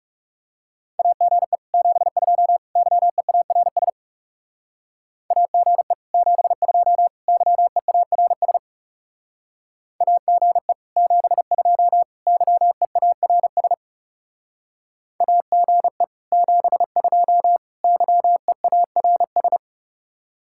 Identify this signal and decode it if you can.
Morse code, heard as a keyed tone, about 40 wpm, 700 Hz. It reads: AGE 72 YEARS AGE 72 YEARS AGE 72 YEARS AGE 72 YEARS